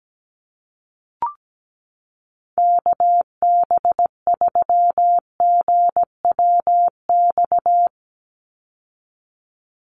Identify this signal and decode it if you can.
Morse code, heard as a keyed tone, about 17 wpm, 700 Hz. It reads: KB3GWX
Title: USA Call Signs 17wpm